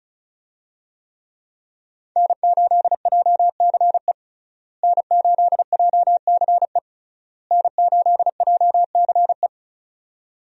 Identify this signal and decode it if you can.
Morse code, heard as a keyed tone, about 36 wpm, 700 Hz. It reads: N8JCE N8JCE N8JCE
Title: USA Call Signs - 5 Characters - Repeated 3x - 35wpm